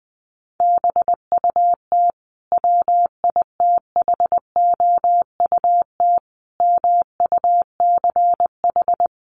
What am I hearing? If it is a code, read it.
Morse code, heard as a keyed tone, about 20 wpm, 700 Hz. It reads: BUT WITHOUT MUCH
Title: Sentences from Top 900 Words 20wpm